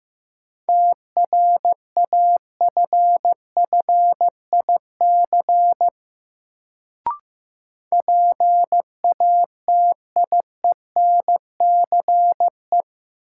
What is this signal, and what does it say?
Morse code, heard as a keyed tone, about 15 wpm, 700 Hz. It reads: TRAFFIC PATIENCE
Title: Sentences from Top 1300 Words - New Word Drill - 15wpm